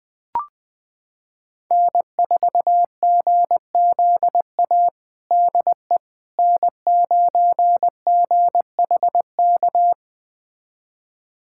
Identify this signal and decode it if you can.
Morse code, heard as a keyed tone, about 20 wpm, 700 Hz. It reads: N4GZA DE N9GHK
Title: USA Callsign DE Callsign - 5 Characters - 20wpm